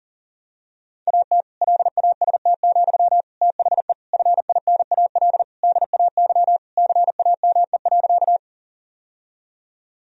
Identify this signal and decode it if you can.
Morse code, heard as a keyed tone, about 40 wpm, 700 Hz. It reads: AT LAST, THE FINAL DAY CAME.
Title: Sentences from Top 1000 Words - Review Part 2 - 40wpm